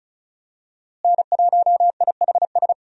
Morse code, heard as a keyed tone, about 36 wpm, 700 Hz. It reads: N1IHS
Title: USA Call Signs - 5 Characters - Repeated 3x - 35wpm